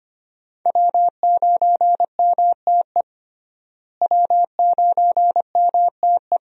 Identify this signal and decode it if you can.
Morse code, heard as a keyed tone, about 25 wpm, 700 Hz. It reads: W9MTE W9MTE
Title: USA Call Signs - 5 Characters - Repeated 2x - 25wpm